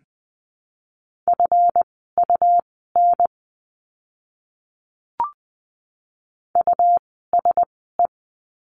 Morse code, heard as a keyed tone, about 20 wpm, spaced as for 14 wpm, 700 Hz. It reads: FUN USE
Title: 139 Lesson 23f - Letter f - Words 20wpm